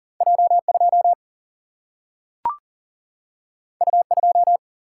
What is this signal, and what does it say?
Morse code, heard as a keyed tone, about 40 wpm, 700 Hz. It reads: J 2 U 2
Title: Single-Letter Single-Number 40wpm